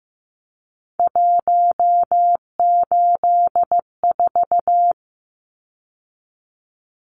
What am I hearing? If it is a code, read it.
Morse code, heard as a keyed tone, about 15 wpm, 700 Hz. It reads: 184
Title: Numbers - 3-Digit 15wpm